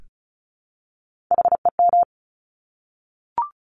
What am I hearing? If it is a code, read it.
Morse code, heard as a keyed tone, about 36 wpm, 700 Hz. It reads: HEM
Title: Top Three Letter Combinations in Words 35wpm